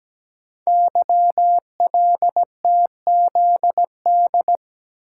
Morse code, heard as a keyed tone, about 17 wpm, 700 Hz. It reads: YLTZD
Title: Code Groups - 5 Letter:Numbers - 17wpm